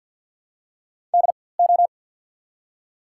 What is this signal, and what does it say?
Morse code, heard as a keyed tone, about 54 wpm, spaced as for 24 wpm, 700 Hz. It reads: DO
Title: 4x Character Spacing - Top 300 Words - 50wpm